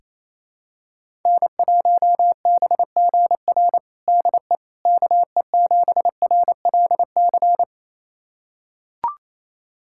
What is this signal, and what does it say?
Morse code, heard as a keyed tone, about 28 wpm, 700 Hz. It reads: N1BGR DE KE7RLC
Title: USA Callsign DE Callsign 28wpm